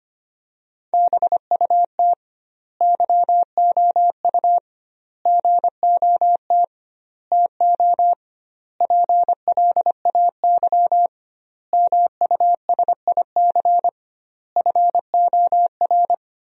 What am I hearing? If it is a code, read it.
Morse code, heard as a keyed tone, about 25 wpm, 700 Hz. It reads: BUT YOU GOT TO PLAY MUSIC FOR
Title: 2x Word-Spacing - Sentences from Top 300 Words - 25wpm